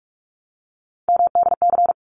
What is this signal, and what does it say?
Morse code, heard as a keyed tone, about 46 wpm, 700 Hz. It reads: MDC
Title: ARRL Sections 45wpm